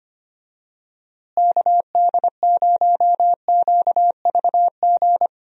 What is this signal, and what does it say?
Morse code, heard as a keyed tone, about 25 wpm, 700 Hz. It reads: KD0QVG
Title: USA Call Signs - 6 Characters - Repeated 3x - 25wpm